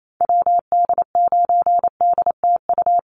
Morse code, heard as a keyed tone, about 28 wpm, 700 Hz. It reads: WD9DTU
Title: USA Call Signs Repeated 3x 28wpm